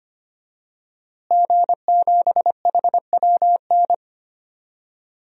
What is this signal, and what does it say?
Morse code, heard as a keyed tone, about 25 wpm, 700 Hz. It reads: G7HWN
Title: DXCC-100 Call Signs - 5 Characters - Repeated 2x - 25wpm